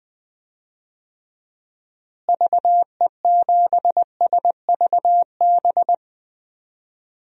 Morse code, heard as a keyed tone, about 21 wpm, 700 Hz. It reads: VE7SVB
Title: DXCC-100 Call Signs - 6 Characters - Repeated 2x - 20wpm